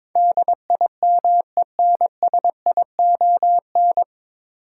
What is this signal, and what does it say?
Morse code, heard as a keyed tone, about 22 wpm, 700 Hz. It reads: DIMENSION
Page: Morse Code Ninja practice 22wpm